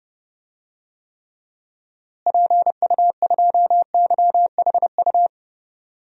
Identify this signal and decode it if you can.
Morse code, heard as a keyed tone, about 30 wpm, 700 Hz. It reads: PU2YHU
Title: DXCC-100 Call Signs - 6 Characters - Repeated 2x - 30wpm